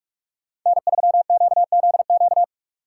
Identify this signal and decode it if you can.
Morse code, heard as a keyed tone, about 44 wpm, 700 Hz. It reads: N3QZQ
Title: USA Call Signs - 5 Characters - 6x Character Spacing - 45wpm